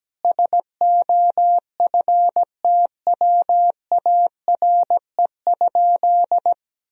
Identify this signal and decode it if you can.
Morse code, heard as a keyed tone, about 17 wpm, 700 Hz. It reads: SOFTWARE?
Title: Sentences from Top 1000 Words - Review Part 3 - 17wpm